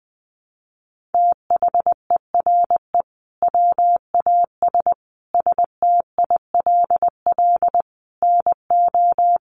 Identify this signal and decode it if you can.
Morse code, heard as a keyed tone, about 20 wpm, 700 Hz. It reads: THERE WAS STILL NO
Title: Sentences from Top 1400 Words 20wpm